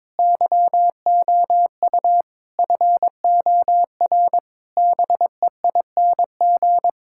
Morse code, heard as a keyed tone, about 22 wpm, 700 Hz. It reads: YOU FOR BEING
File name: practice / 22wpm / Sets of 7 Words - 22wpm